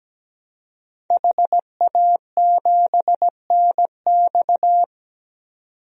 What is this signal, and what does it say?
Morse code, heard as a keyed tone, about 17 wpm, 700 Hz. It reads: HA7NX
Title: DXCC-100 Call Signs - 5 Characters - 17wpm